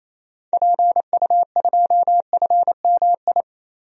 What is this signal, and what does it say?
Morse code, heard as a keyed tone, about 28 wpm, 700 Hz. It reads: PU2FMI
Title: DXCC-100 Call Signs - 6 Characters - Repeated 3x - 28wpm